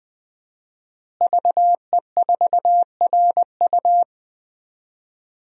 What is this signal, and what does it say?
Morse code, heard as a keyed tone, about 20 wpm, 700 Hz. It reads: VE4RU
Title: DXCC-100 Call Signs - 5 Characters - Repeated 2x - 20wpm